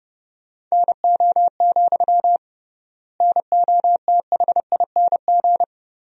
Morse code, heard as a keyed tone, about 32 wpm, 700 Hz. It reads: NO, NOTHING
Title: 3x Word-Spacing - Sentences from Top 500 Words - 30wpm